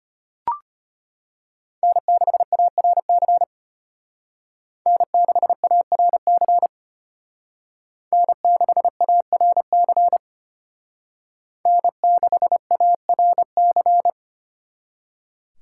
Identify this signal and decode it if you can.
Morse code, heard as a keyed tone, about 33 wpm, 700 Hz. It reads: N6ARC N6ARC N6ARC N6ARC